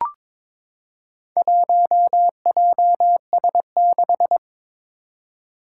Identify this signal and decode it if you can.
Morse code, heard as a keyed tone, about 22 wpm, 700 Hz. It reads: 1JS6
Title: Four Challenging Letter-Number Combinations 22wpm